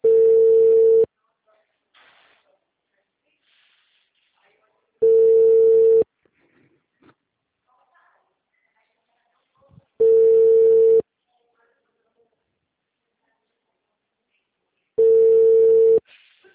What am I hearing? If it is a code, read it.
Ringback tone (China, 450 Hz), with speech.